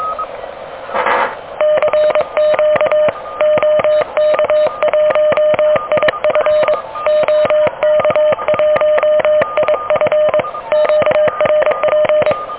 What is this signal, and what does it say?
Morse code, heard as a keyed tone, about 23 wpm, 610 Hz. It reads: CQ OK1IF OK1IF QRP
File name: OK QRP 2023 / OK QRP 2023